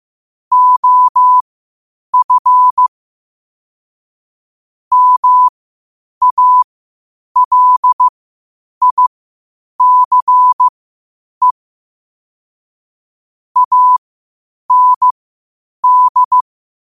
Morse code, heard as a keyed tone, about 15 wpm, spaced as for 8 wpm, 1000 Hz. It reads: OF MALICE AND
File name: qotdfiles / 8wpm